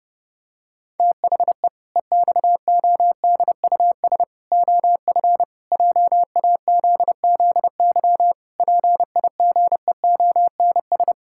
Morse code, heard as a keyed tone, about 30 wpm, 700 Hz. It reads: THE EXODUS OF JAZZY PIGEONS
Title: Pangram Challenge v2 - 30wpm